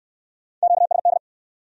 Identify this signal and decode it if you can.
Morse code, heard as a keyed tone, about 60 wpm, 700 Hz. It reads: QSD